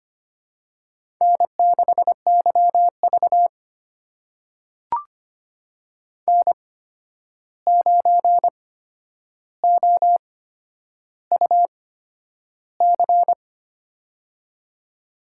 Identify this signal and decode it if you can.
Morse code, heard as a keyed tone, about 25 wpm, spaced as for 10 wpm, 700 Hz. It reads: N6YV N9OUC
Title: USA Call Signs - 8x Character Spacing - 25wpm